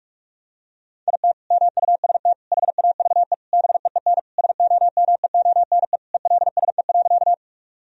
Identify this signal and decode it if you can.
Morse code, heard as a keyed tone, about 41 wpm, 700 Hz. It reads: IT MUST HAVE BEEN SOMEONE ELSE.
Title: Sentences from Top 1000 Words - Review Part 4 - 45wpm